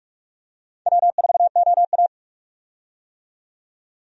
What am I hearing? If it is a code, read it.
Morse code, heard as a keyed tone, about 48 wpm, 700 Hz. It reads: W4OA
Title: USA Call Signs - 4 Characters - 45wpm